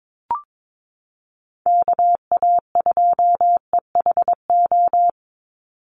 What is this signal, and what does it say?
Morse code, heard as a keyed tone, about 22 wpm, 700 Hz. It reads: KA2EHO
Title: USA Call Signs 22wpm